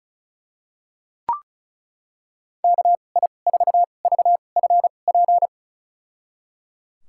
Morse code, heard as a keyed tone, about 36 wpm, spaced as for 25 wpm, 700 Hz. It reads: KI4VFP
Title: USA Call Signs - 6 Characters - 2x Character Spacing - 35wpm